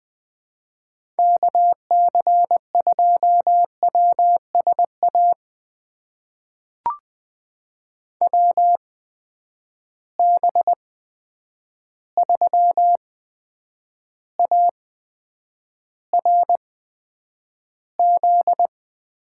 Morse code, heard as a keyed tone, about 20 wpm, spaced as for 9 wpm, 700 Hz. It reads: KC2WSA WB3ARZ
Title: USA Call Signs - 6 Characters - 8x Character Spacing - 20wpm